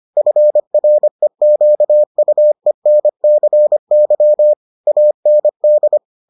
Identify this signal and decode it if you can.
Morse code, heard as a keyed tone, about 25 wpm, 600 Hz. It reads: FREQUENCY AND